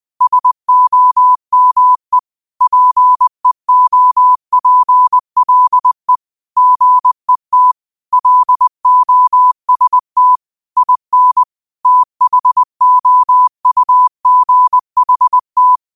Morse code, heard as a keyed tone, about 20 wpm, 1000 Hz. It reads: SOME PEOPLE GET LOST IN THOUGHT
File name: qotdfiles / 20wpm